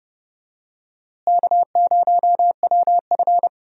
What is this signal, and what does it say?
Morse code, heard as a keyed tone, about 32 wpm, 700 Hz. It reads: K0WF
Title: USA Call Signs - 4 Characters - Repeated 3x - 30wpm